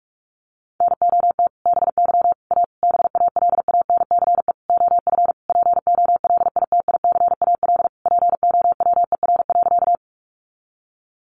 Morse code, heard as a keyed tone, about 44 wpm, 700 Hz. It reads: NOT BY A BALANCE OF POLITICAL POWER.
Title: Sentences from Top 900 Words 45wpm